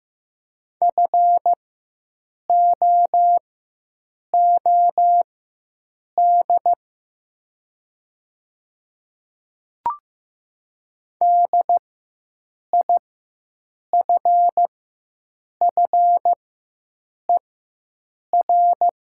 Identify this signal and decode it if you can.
Morse code, heard as a keyed tone, about 15 wpm, spaced as for 7 wpm, 700 Hz. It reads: FOOD DIFFER
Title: 4x Character Spacing - Top 500 Words - 15wpm